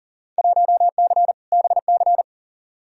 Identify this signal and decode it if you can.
Morse code, heard as a keyed tone, about 40 wpm, 700 Hz. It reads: 1C BC
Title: Field Day Exchanges Encore - 40wpm